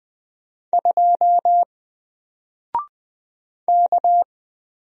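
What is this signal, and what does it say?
Morse code, heard as a keyed tone, about 20 wpm, 700 Hz. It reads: 2 K
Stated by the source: Single letter and number Morse drill at 20 wpm, from N4LCD.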